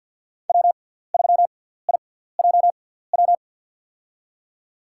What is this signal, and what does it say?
Morse code, heard as a keyed tone, about 48 wpm, spaced as for 17 wpm, 700 Hz. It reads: W3IJW
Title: USA Call Signs - 5 Characters - 6x Character Spacing - 50wpm